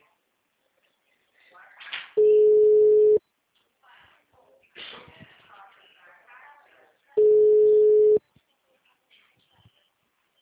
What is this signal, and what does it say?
Ringback tone (Europe, 425 Hz), with speech.